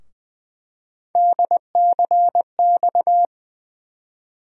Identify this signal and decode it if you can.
Morse code, heard as a keyed tone, about 21 wpm, 700 Hz. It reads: DCX
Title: USA Call Sign Postfixes - 3 Letters 20wpm